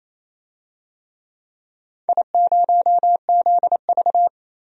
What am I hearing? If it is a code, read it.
Morse code, heard as a keyed tone, about 28 wpm, 700 Hz. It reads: I0ZV
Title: DXCC-100 Call Signs - 4 Characters - Repeated 2x - 28wpm